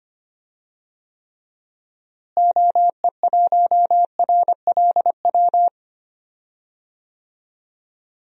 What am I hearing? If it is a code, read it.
Morse code, heard as a keyed tone, about 25 wpm, 700 Hz. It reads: OE1RLW
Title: DXCC-100 Call Signs - 6 Characters - Repeated 3x - 25wpm